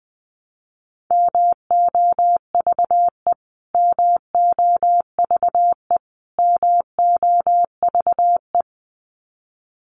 Morse code, heard as a keyed tone, about 20 wpm, 700 Hz. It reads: MOVE MOVE MOVE
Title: Top 200 Words Repeated 20wpm